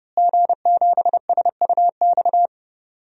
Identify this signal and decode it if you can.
Morse code, heard as a keyed tone, about 30 wpm, 700 Hz. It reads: G7SUX
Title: DXCC-100 Call Signs - 5 Characters - Repeated 2x - 30wpm